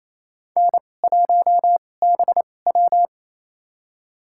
Morse code, heard as a keyed tone, about 28 wpm, spaced as for 20 wpm, 700 Hz. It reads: N1BW
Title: USA Call Signs - 4 Characters - 2x Character Spacing - 28wpm